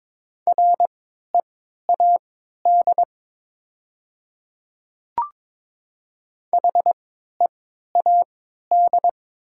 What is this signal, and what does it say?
Morse code, heard as a keyed tone, about 23 wpm, spaced as for 13 wpm, 700 Hz. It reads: READ HEAD
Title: 3x Character Spacing - Top 300 Words - 22wpm